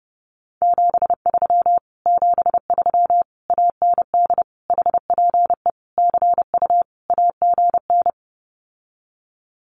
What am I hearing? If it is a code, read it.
Morse code, heard as a keyed tone, about 30 wpm, 700 Hz. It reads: 73 73 AND HPE CU AGN
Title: Common QSO Phrases Encore 30wpm